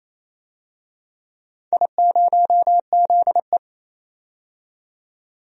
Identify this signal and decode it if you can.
Morse code, heard as a keyed tone, about 28 wpm, 700 Hz. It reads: I0ZE
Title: DXCC-100 Call Signs - 4 Characters - 28wpm